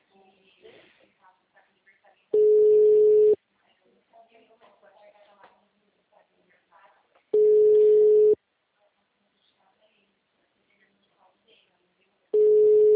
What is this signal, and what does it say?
Ringback tone (Europe, 425 Hz), with speech.
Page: Bebedouro - SP